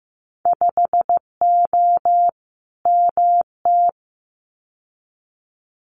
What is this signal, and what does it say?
Morse code, heard as a keyed tone, about 15 wpm, 700 Hz. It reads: 5O MT
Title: Winter Field Day Exchanges Encore 15wpm